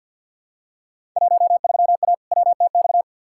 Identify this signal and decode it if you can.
Morse code, heard as a keyed tone, about 47 wpm, 700 Hz. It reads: 13A WTX
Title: Field Day Exchanges Encore - 50wpm